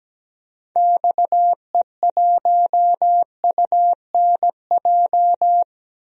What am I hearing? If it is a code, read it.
Morse code, heard as a keyed tone, about 17 wpm, 700 Hz. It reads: XE1UNJ
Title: DXCC-100 Call Signs - 6 Characters - 17wpm